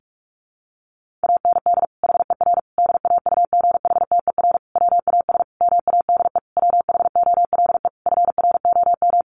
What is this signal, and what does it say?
Morse code, heard as a keyed tone, about 44 wpm, 700 Hz. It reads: AND HER DAUGHTER WAS MADE WHOLE FROM
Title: Sentences from Top 1500 Words 45wpm